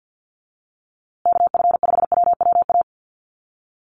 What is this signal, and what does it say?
Morse code, heard as a keyed tone, about 48 wpm, 700 Hz. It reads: KF5WWA
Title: USA Call Signs Repeated 3x 50wpm